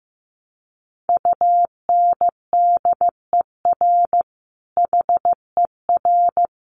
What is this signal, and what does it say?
Morse code, heard as a keyed tone, about 15 wpm, 700 Hz. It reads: UNDER HER
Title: Sentences from Top 1300 Words 15wpm